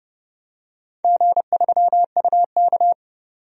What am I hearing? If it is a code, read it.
Morse code, heard as a keyed tone, about 30 wpm, 700 Hz. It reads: G3UK
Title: DXCC-100 Call Signs - 4 Characters - 30wpm